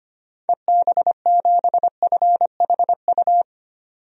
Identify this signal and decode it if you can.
Morse code, heard as a keyed tone, about 25 wpm, 700 Hz. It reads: EB7FHU
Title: DXCC-100 Call Signs - 6 Characters - 25wpm